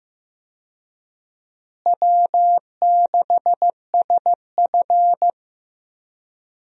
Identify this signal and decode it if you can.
Morse code, heard as a keyed tone, about 15 wpm, 700 Hz. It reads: W6SF
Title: USA Call Signs - 4 Characters - Repeated 3x - 15wpm